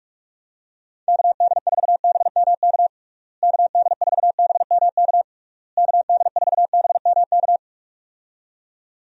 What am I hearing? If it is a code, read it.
Morse code, heard as a keyed tone, about 43 wpm, 700 Hz. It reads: KD4BMK KD4BMK KD4BMK
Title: USA Call Signs - 6 Characters - Repeated 3x - 45wpm